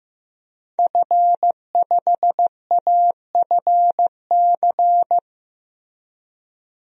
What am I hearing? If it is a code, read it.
Morse code, heard as a keyed tone, about 15 wpm, 700 Hz. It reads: F5AFC
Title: DXCC-100 Call Signs - 5 Characters - Repeated 3x - 15wpm